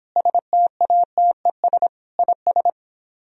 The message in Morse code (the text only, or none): STATES IS